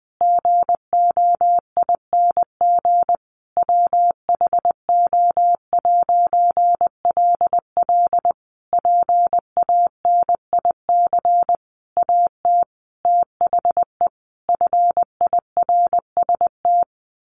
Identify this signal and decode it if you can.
Morse code, heard as a keyed tone, about 20 wpm, 700 Hz. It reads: GOING WHO'LL PANIC AT THE FIRST
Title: A Sound Of Thunder 20wpm